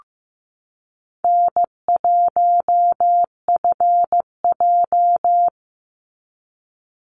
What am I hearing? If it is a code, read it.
Morse code, heard as a keyed tone, about 15 wpm, 700 Hz. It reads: N1FJ
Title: NA SOTA Activator Callsigns 15wpm